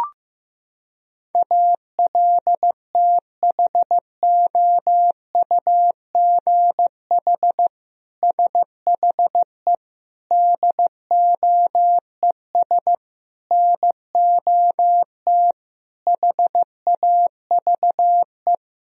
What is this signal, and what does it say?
Morse code, heard as a keyed tone, about 15 wpm, 700 Hz. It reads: ALTHOUGH SHE DOES NOT HAVE
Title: Sentences from Top 1000 Words - Review Part 3 - 15wpm